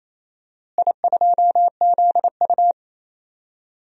Morse code, heard as a keyed tone, about 28 wpm, 700 Hz. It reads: I2ZU
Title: DXCC-100 Call Signs - 4 Characters - Repeated 3x - 28wpm